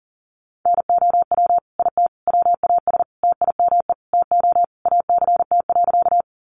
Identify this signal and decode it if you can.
Morse code, heard as a keyed tone, about 40 wpm, 700 Hz. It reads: NOW IT WAS TIME TO ACT.
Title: Sentences from Top 200 Words 40wpm